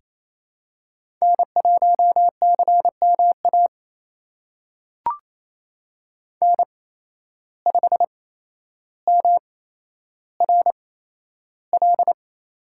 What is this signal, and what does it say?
Morse code, heard as a keyed tone, about 28 wpm, spaced as for 12 wpm, 700 Hz. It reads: N1CMA N5MRL
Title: USA Call Signs - 5 Characters - 8x Character Spacing - 28wpm